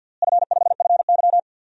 Morse code, heard as a keyed tone, about 52 wpm, 700 Hz. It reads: F5FY
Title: DXCC-100 Call Signs - 4 Characters - Repeated 3x - 50wpm